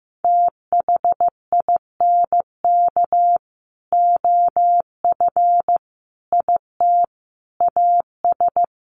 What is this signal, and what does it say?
Morse code, heard as a keyed tone, about 15 wpm, 700 Hz. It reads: THINK OF IT AS
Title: Sets of 5 Words Encore2 15wpm